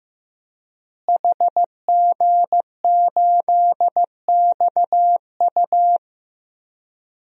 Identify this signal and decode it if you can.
Morse code, heard as a keyed tone, about 16 wpm, 700 Hz. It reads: HG8XU
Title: DXCC-100 Call Signs - 5 Characters - Repeated 3x - 15wpm